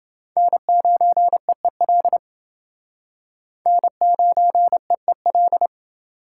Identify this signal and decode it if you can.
Morse code, heard as a keyed tone, about 29 wpm, 700 Hz. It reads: N9EEL N9EEL